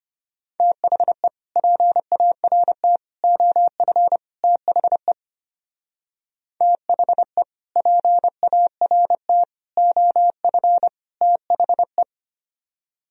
Morse code, heard as a keyed tone, about 30 wpm, 700 Hz. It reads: THE PART OF THE THE PART OF THE
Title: Speed-Racing - Sets of 4 Words 25wpm